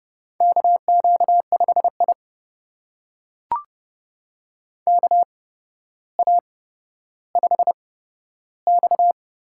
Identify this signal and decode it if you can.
Morse code, heard as a keyed tone, about 30 wpm, spaced as for 13 wpm, 700 Hz. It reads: KQ5I KA5X
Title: USA Call Signs - 4 Characters - 8x Character Spacing - 30wpm